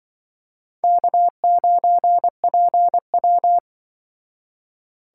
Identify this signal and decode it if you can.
Morse code, heard as a keyed tone, about 25 wpm, 700 Hz. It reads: K9PW
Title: Speed Racing - USA 1x2 Call Signs 20wpm